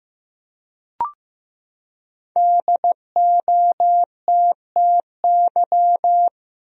Morse code, heard as a keyed tone, about 15 wpm, 700 Hz. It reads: DOTTY
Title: USA Common Names - 15wpm